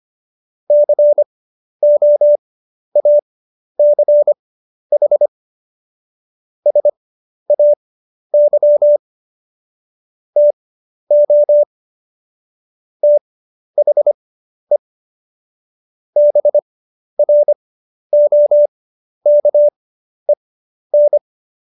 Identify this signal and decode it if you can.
Morse code, heard as a keyed tone, about 25 wpm, spaced as for 11 wpm, 590 Hz. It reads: COACH SAY TO THE BROKEN
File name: Jokes-25-char-10-eff